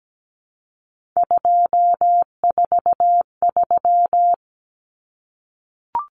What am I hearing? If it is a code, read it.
Morse code, heard as a keyed tone, about 17 wpm, 700 Hz. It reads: 243
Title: Numbers - 3-Digit 17wpm